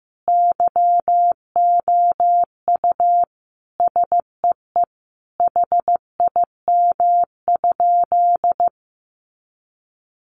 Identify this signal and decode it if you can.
Morse code, heard as a keyed tone, about 15 wpm, 700 Hz. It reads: YOU SEE HIM?
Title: Sentences from Top 100 Words 15wpm